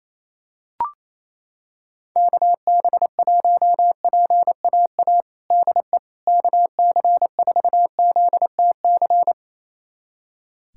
Morse code, heard as a keyed tone, about 28 wpm, 700 Hz. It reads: KB1PAA DE KC4ZTC
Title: USA Callsign DE Callsign - 6 Characters - 28wpm